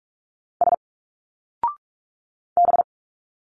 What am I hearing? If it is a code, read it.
Morse code, heard as a keyed tone, about 46 wpm, 700 Hz. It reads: S B
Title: Single Letters 45wpm